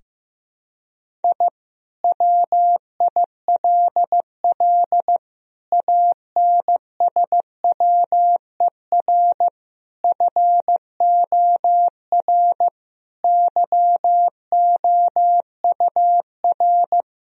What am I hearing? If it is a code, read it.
Morse code, heard as a keyed tone, about 15 wpm, 700 Hz. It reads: I WILL ANSWER FOR YOUR
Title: Sentences from Top 1000 Words - Review Part 4 - 15wpm